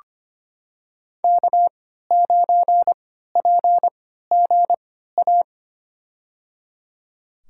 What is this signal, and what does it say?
Morse code, heard as a keyed tone, about 25 wpm, spaced as for 14 wpm, 700 Hz. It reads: K9PGA
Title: USA Call Signs - 5 Characters - 3x Character Spacing - 25wpm